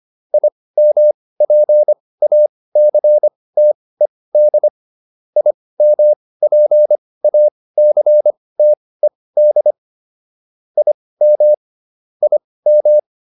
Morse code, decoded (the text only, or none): IMPACTED IMPACTED IM IM